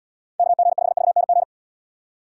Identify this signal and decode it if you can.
Morse code, heard as a keyed tone, about 78 wpm, 700 Hz. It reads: KG5VNG